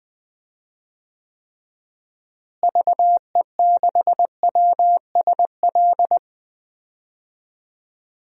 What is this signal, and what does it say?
Morse code, heard as a keyed tone, about 19 wpm, 700 Hz. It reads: VE6WSL
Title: DXCC-100 Call Signs - 6 Characters - Repeated 2x - 20wpm